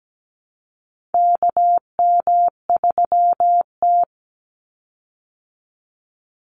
Morse code, heard as a keyed tone, about 17 wpm, 700 Hz. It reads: KM3T
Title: Call Signs Repeated 2x - 17wpm